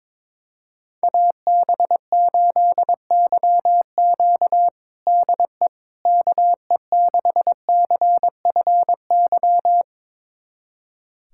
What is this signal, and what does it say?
Morse code, heard as a keyed tone, about 21 wpm, 700 Hz. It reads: AB8YQ DE KE6CFY
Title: USA Callsign DE Callsign 22wpm